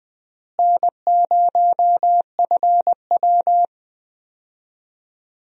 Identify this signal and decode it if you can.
Morse code, heard as a keyed tone, about 20 wpm, 700 Hz. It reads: N0FW
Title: USA Call Signs - 4 Characters - Repeated 2x - 20wpm